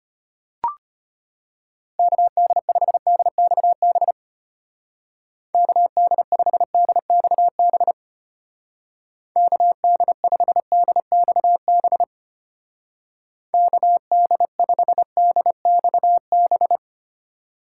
Morse code, heard as a keyed tone, about 31 wpm, 700 Hz. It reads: KD5DXB KD5DXB KD5DXB KD5DXB
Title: Speed Racing - USA 2x3 Call Signs 25wpm